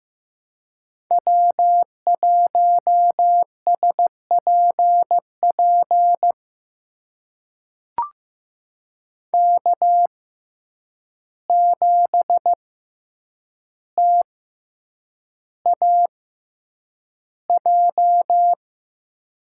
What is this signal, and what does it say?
Morse code, heard as a keyed tone, about 15 wpm, spaced as for 8 wpm, 700 Hz. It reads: W1SPP K7TAJ
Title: USA Call Signs - 5 Characters - 6x Character Spacing - 15wpm